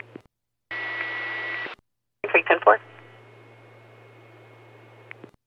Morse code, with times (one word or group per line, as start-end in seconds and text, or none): none